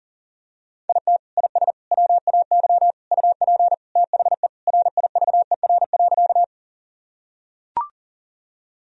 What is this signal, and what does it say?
Morse code, heard as a keyed tone, about 40 wpm, 700 Hz. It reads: IT IS WAY UP THE RIVER.
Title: Sentences from Top 1000 Words - Review Part 1 - 40wpm